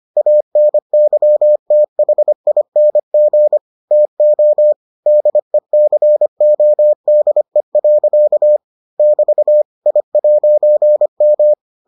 Morse code, heard as a keyed tone, about 25 wpm, 600 Hz. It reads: ANYTHING TO DECODE. <BT> I'M